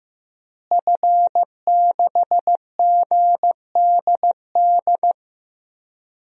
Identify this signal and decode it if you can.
Morse code, heard as a keyed tone, about 15 wpm, 700 Hz. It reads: F6GDD
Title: DXCC-100 Call Signs - 5 Characters - Repeated 3x - 15wpm